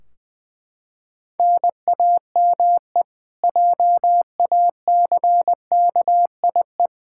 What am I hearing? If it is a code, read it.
Morse code, heard as a keyed tone, about 20 wpm, 700 Hz. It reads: NAME JACKIE
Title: Practice QSO - Exchange 1 - Beginning - 20wpm